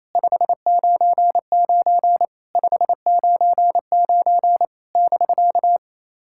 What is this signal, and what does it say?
Morse code, heard as a keyed tone, about 29 wpm, 700 Hz. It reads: 599 599 <BK>
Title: SOTA QSO Practice 28wpm